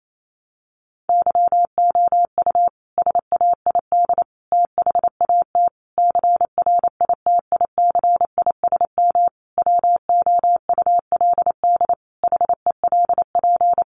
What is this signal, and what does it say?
Morse code, heard as a keyed tone, about 28 wpm, 700 Hz. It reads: YOU SAID THAT CRITICISM WOULD HELP
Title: Sentences from Top 1300 Words 28wpm